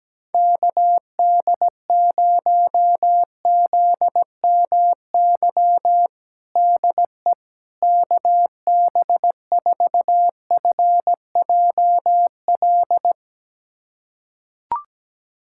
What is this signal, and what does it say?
Morse code, heard as a keyed tone, about 17 wpm, 700 Hz. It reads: KD0ZMY DE KB4FJL
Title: USA Callsign DE Callsign - 6 Characters - 17wpm